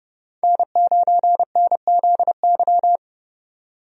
Morse code, heard as a keyed tone, about 30 wpm, 700 Hz. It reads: N9NZY
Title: USA Call Signs - 5 Characters - 30wpm